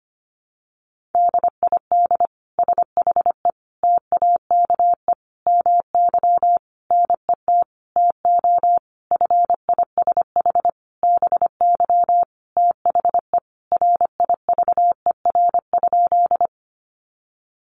Morse code, heard as a keyed tone, about 25 wpm, 700 Hz. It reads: DID SHE TAKE MY NET TO FISH BY THE RIVER?
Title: Sentences from Top 1400 Words 25wpm